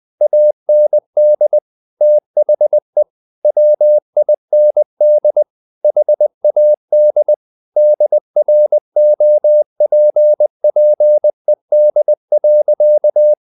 Morse code, heard as a keyed tone, about 20 wpm, 600 Hz. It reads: AND THE WIND HAD DROPPED.
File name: the old man and the sea part 2 20wpm